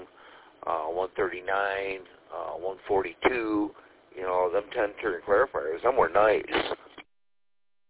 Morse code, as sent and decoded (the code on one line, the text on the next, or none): none